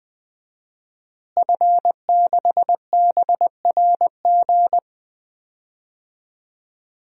Morse code, heard as a keyed tone, about 20 wpm, 700 Hz. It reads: F6BRG
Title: DXCC-100 Call Signs - 5 Characters - Repeated 2x - 20wpm